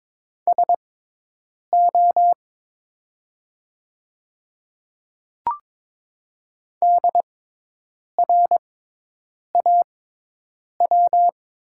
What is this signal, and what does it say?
Morse code, heard as a keyed tone, about 22 wpm, spaced as for 8 wpm, 700 Hz. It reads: SO DRAW
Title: 6x Character Spacing - Top 300 Words - 22wpm